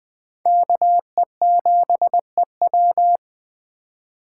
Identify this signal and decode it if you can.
Morse code, heard as a keyed tone, about 20 wpm, 700 Hz. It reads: KE7EW
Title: Speed Racing - USA 2x2 Call Signs 17wpm